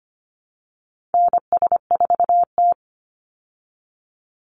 Morse code, heard as a keyed tone, about 25 wpm, 700 Hz. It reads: NS4T
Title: Call Signs Repeated 3x - 25wpm